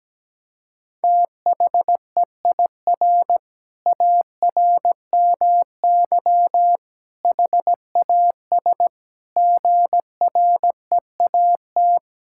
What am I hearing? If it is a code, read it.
Morse code, heard as a keyed tone, about 17 wpm, 700 Hz. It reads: THEIR ARMY HAS GREAT
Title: Sentences from Top 1000 Words - Review Part 5 - 17wpm